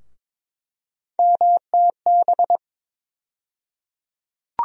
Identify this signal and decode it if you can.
Morse code, heard as a keyed tone, about 21 wpm, 700 Hz. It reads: MTB
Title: USA Call Sign Postfixes - 3 Letters 22wpm